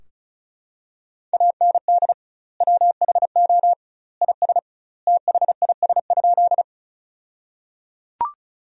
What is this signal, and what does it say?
Morse code, heard as a keyed tone, about 36 wpm, 700 Hz. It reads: AND WHO IS THIS?
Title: 2x Word-Spacing - Sentences from Top 100 Words - 35wpm